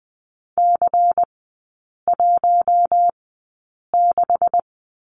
Morse code, heard as a keyed tone, about 20 wpm, spaced as for 8 wpm, 700 Hz. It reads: C16
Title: 152 Lesson 25e - Letter p - Three Characters 20wpm